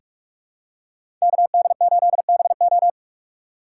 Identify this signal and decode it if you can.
Morse code, heard as a keyed tone, about 44 wpm, 690 Hz. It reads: KD8BO
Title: USA Call Signs - 5 Characters - 6x Character Spacing - 45wpm